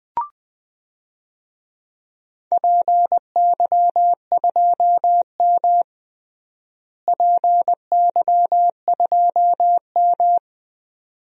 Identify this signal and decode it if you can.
Morse code, heard as a keyed tone, about 20 wpm, 700 Hz. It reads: PY2M PY2M
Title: DXCC-100 Call Signs - 4 Characters - Repeated 2x - 20wpm